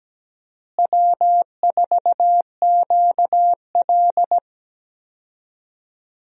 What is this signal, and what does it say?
Morse code, heard as a keyed tone, about 17 wpm, 700 Hz. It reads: W4QL
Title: USA Call Signs - 4 Characters - Repeated 3x - 17wpm